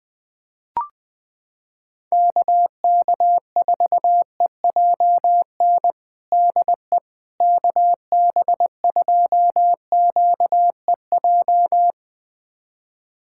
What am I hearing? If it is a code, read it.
Morse code, heard as a keyed tone, about 21 wpm, 700 Hz. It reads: KK4EJN DE KB2QEJ
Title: USA Callsign DE Callsign 20wpm